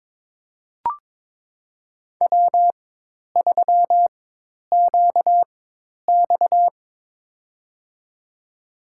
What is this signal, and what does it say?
Morse code, heard as a keyed tone, about 22 wpm, spaced as for 10 wpm, 700 Hz. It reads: W3QX
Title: USA Call Signs - 4 Characters - 4x Character Spacing - 22wpm